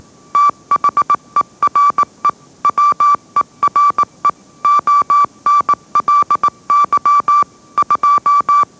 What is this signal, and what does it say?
Morse code, heard as a keyed tone, about 18 wpm, 1160 Hz. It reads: THERE WERE ONLY 2
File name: mrx / mrx081 20wpm